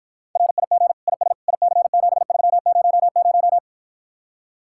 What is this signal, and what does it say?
Morse code, heard as a keyed tone, about 54 wpm, 700 Hz. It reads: RIG IS IC7300